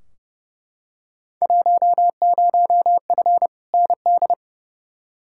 Morse code, heard as a keyed tone, about 30 wpm, 700 Hz. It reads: 10F ND
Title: Field Day Exchanges Encore - 30wpm